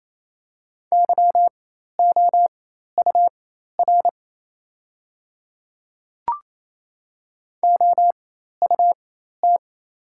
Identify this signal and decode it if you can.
Morse code, heard as a keyed tone, about 28 wpm, spaced as for 13 wpm, 690 Hz. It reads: YOUR OUT
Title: 4x Character Spacing - Top 100 Words - 28wpm